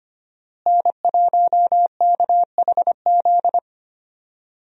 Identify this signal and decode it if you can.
Morse code, heard as a keyed tone, about 25 wpm, 700 Hz. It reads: N1KHZ
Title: USA Call Signs - 5 Characters - Repeated 2x - 25wpm